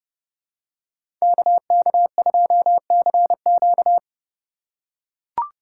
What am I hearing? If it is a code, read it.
Morse code, heard as a keyed tone, about 32 wpm, 700 Hz. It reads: KK2CQ
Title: USA Call Signs - 5 Characters - 2x Character Spacing - 30wpm